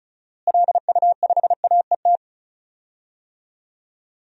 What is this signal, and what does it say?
Morse code, heard as a keyed tone, about 34 wpm, 700 Hz. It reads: LU5AET
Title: DXCC-100 Call Signs - 6 Characters - Repeated 2x - 35wpm